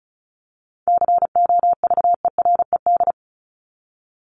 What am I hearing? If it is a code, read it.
Morse code, heard as a keyed tone, about 34 wpm, 700 Hz. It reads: COVERED
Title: ED Suffix - Top 100 Words with Base Words 35wpm